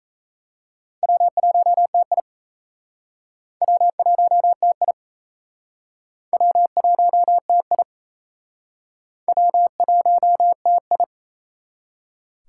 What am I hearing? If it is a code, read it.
Morse code, heard as a keyed tone, about 35 wpm, 700 Hz. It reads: W1TI W1TI W1TI W1TI
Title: Speed Racing - USA Call Signs 28wpm